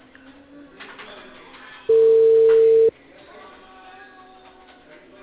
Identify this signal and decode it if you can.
Ringback tone (China, 450 Hz).